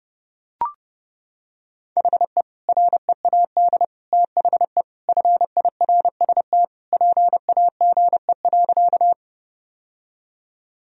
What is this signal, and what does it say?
Morse code, heard as a keyed tone, about 30 wpm, 700 Hz. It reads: HE READ THE FIRST PAGE.
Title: Sentences from Top 1000 Words - Review Part 1 - 30wpm